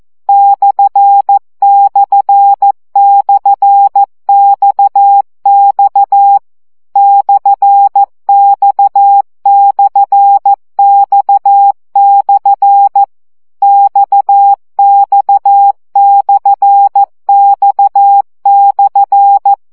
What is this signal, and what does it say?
Morse code, heard as a keyed tone, about 14 wpm, 800 Hz. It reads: ///XX /X/X/ XX/X/
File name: CW gyakorlas x per 12 baud